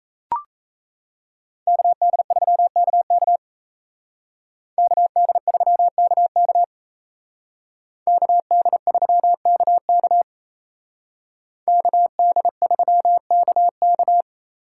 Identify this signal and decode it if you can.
Morse code, heard as a keyed tone, about 36 wpm, 700 Hz. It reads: KD3KK KD3KK KD3KK KD3KK